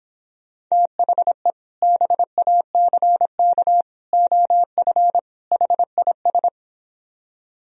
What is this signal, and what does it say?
Morse code, heard as a keyed tone, about 26 wpm, 700 Hz. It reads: THE BACK OF HIS